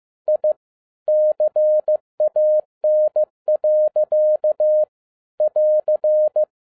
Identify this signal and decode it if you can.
Morse code, heard as a keyed tone, about 15 wpm, 610 Hz. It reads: I CAN. <AR>
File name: 2011-07-mp3 / n12 T8XO 15wpm